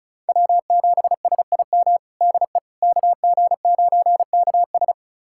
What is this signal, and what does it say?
Morse code, heard as a keyed tone, about 34 wpm, 700 Hz. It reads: W7SIM DE KG9KS
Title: USA Callsign DE Callsign - 5 Characters - 35wpm